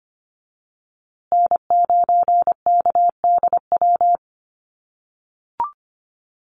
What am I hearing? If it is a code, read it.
Morse code, heard as a keyed tone, about 25 wpm, 700 Hz. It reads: N9KDW
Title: USA Call Signs Repeated 3x 25wpm